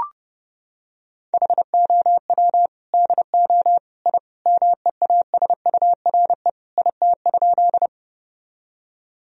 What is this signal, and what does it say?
Morse code, heard as a keyed tone, about 30 wpm, 700 Hz. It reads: HOW DO I MEASURE IT?
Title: Sentences from Top 1000 Words - Review Part 2 - 30wpm